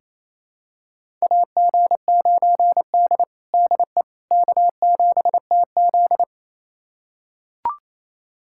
Morse code, heard as a keyed tone, about 28 wpm, 700 Hz. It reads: AG9D DE K7TZ
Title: USA Callsign DE Callsign - 4 Characters - 28wpm